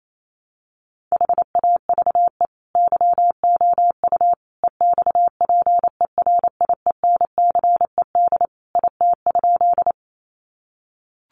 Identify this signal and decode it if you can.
Morse code, heard as a keyed tone, about 28 wpm, 700 Hz. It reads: HAVE YOU EXPERIENCED IT?
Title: Sentences from Top 1100 Words 28wpm